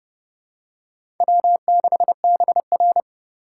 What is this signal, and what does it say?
Morse code, heard as a keyed tone, about 32 wpm, 700 Hz. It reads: W6BR
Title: USA Call Signs - 4 Characters - 2x Character Spacing - 30wpm